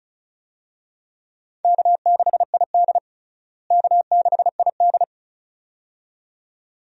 Morse code, heard as a keyed tone, about 34 wpm, 700 Hz. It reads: K6ID K6ID
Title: USA Call Signs - 4 Characters - Repeated 2x - 35wpm